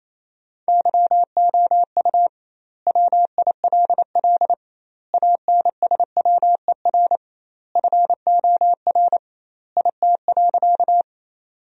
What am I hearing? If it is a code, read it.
Morse code, heard as a keyed tone, about 28 wpm, 700 Hz. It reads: YOU WILL ANSWER FOR IT.
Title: 2x Word-Spacing - Sentences from Top 300 Words - 28wpm